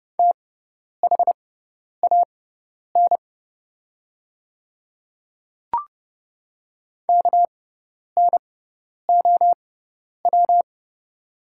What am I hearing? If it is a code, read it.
Morse code, heard as a keyed tone, about 30 wpm, spaced as for 10 wpm, 690 Hz. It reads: THAN KNOW
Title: 6x Character Spacing - Top 500 Words - 30wpm